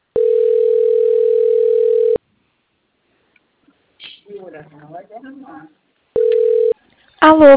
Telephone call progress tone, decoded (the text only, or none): ringback tone